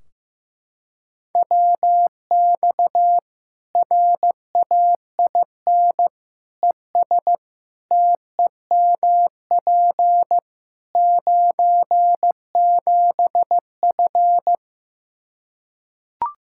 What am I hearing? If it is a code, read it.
Morse code, heard as a keyed tone, about 15 wpm, 700 Hz. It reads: WX RAIN ES TEMP 97F
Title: Practice QSO - Exchange 2 - Info - 15wpm